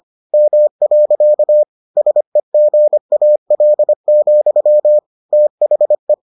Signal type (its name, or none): Morse code